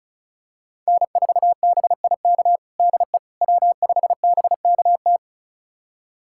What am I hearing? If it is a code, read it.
Morse code, heard as a keyed tone, about 36 wpm, 700 Hz. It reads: N4BIK DE W5BKT
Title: USA Callsign DE Callsign - 5 Characters - 35wpm